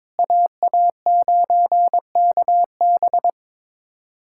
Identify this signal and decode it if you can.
Morse code, heard as a keyed tone, about 22 wpm, 700 Hz. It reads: AA9KB
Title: USA Callsign DE Callsign 22wpm